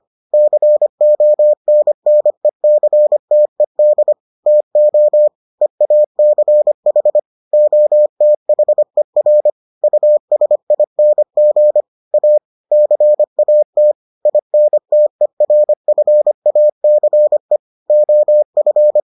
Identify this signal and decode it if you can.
Morse code, heard as a keyed tone, about 25 wpm, 600 Hz. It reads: CONNECTED TO EACH OTHER USING A CAT INTERFACE OF